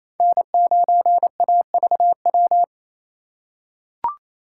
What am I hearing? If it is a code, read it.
Morse code, heard as a keyed tone, about 28 wpm, 700 Hz. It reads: N9AVW
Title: USA Call Signs - 5 Characters - 2x Character Spacing - 28wpm